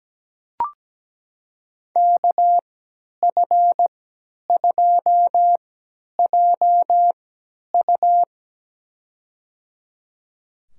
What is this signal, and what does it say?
Morse code, heard as a keyed tone, about 17 wpm, spaced as for 10 wpm, 700 Hz. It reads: KF2JU
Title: USA Call Signs - 3x Character Spacing - 17wpm